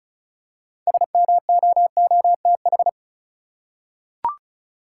Morse code, heard as a keyed tone, about 36 wpm, 700 Hz. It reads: SMOOTH
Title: Six Letter Words - 35wpm